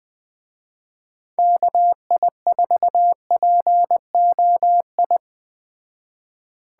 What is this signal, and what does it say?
Morse code, heard as a keyed tone, about 21 wpm, 700 Hz. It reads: KI4POI
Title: Speed Racing - USA Call Signs 17wpm